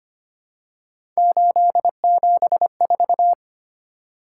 Morse code, heard as a keyed tone, about 25 wpm, 700 Hz. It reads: 874
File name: Three Letter-Number Combinations - 25wpm